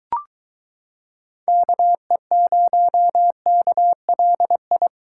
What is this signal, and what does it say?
Morse code, heard as a keyed tone, about 22 wpm, 700 Hz. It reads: KE0KLI